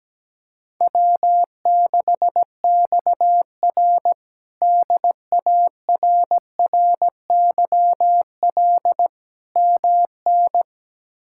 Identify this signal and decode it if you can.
Morse code, heard as a keyed tone, about 17 wpm, 700 Hz. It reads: W6XR DARRYL MN
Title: K1USN SST QSO Practice - 17wpm